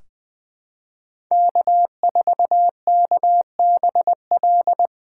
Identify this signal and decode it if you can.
Morse code, heard as a keyed tone, about 20 wpm, 700 Hz. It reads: K4KBL
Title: K1USN SST QSO Practice - 20wpm